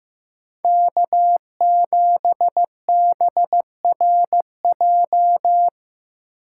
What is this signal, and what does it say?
Morse code, heard as a keyed tone, about 15 wpm, 700 Hz. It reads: K7BRJ
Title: USA Call Signs - 5 Characters - 6x Character Spacing - 15wpm